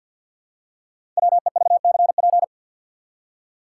Morse code, heard as a keyed tone, about 47 wpm, 700 Hz. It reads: WEVCP
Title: Code Groups - 5 Letter:Numbers - 50wpm